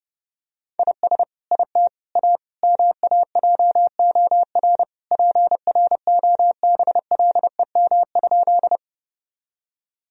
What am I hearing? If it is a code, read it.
Morse code, heard as a keyed tone, about 30 wpm, 700 Hz. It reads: IS IT A MAJOR PROBLEM?
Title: Sentences from Top 1000 Words - Review Part 4 - 30wpm